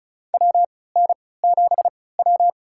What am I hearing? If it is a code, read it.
Morse code, heard as a keyed tone, about 36 wpm, spaced as for 20 wpm, 700 Hz. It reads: WN7W
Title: USA Call Signs - 4 Characters - 3x Character Spacing - 35wpm